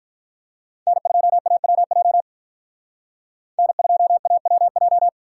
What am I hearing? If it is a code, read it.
Morse code, heard as a keyed tone, about 45 wpm, 700 Hz. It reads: N2AWJ N2AWJ